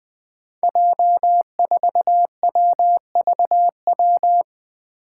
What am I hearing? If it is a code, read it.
Morse code, heard as a keyed tone, about 20 wpm, 700 Hz. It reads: J4WVW
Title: Five Challenging Letter-Number Combinations 20wpm